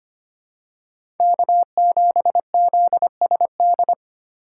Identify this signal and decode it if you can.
Morse code, heard as a keyed tone, about 25 wpm, 700 Hz. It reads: K7ZSD
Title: USA Call Signs - 5 Characters - 4x Character Spacing - 25wpm